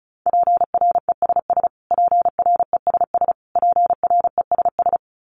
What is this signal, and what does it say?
Morse code, heard as a keyed tone, about 34 wpm, 700 Hz. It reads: PRESS PRESS PRESS
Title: Top 500 Words Repeated 35wpm